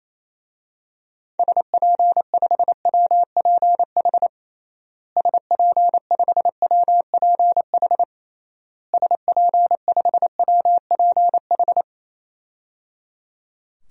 Morse code, heard as a keyed tone, about 28 wpm, 700 Hz. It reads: SP5WPH SP5WPH SP5WPH
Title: DXCC-100 Call Signs - 6 Characters - Repeated 3x - 28wpm